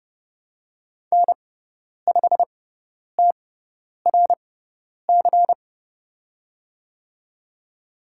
Morse code, heard as a keyed tone, about 31 wpm, spaced as for 10 wpm, 700 Hz. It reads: N5TRC
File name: Lesson 17g - Letter u - Call Signs 30wpm